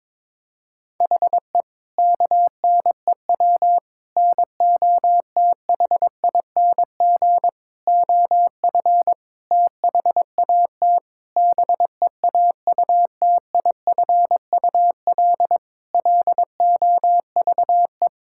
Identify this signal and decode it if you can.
Morse code, heard as a keyed tone, about 22 wpm, 700 Hz. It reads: HE KNEW NOTHING OF THAT BEAUTIFUL LOVE
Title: Sentences from Top 1600 Words 22wpm